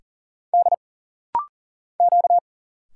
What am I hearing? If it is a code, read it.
Morse code, heard as a keyed tone, about 40 wpm, 700 Hz. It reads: D Q
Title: Single Letter-Number Alt - Rapid-Fire 40wpm